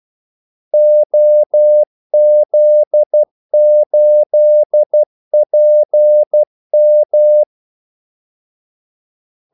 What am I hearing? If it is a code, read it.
Morse code, heard as a keyed tone, about 12 wpm, 600 Hz. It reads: OZ8PM
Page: 2 ekstra ordmellemrum mellem kaldesignalerne